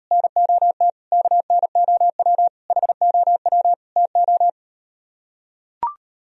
Morse code, heard as a keyed tone, about 38 wpm, 700 Hz. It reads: NOT KNOW HOW TO